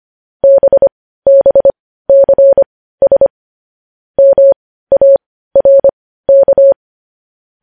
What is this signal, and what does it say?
Morse code, heard as a keyed tone, about 25 wpm, spaced as for 15 wpm, 550 Hz. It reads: BBCS MARK
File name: podcast / News headlines-1x-25x15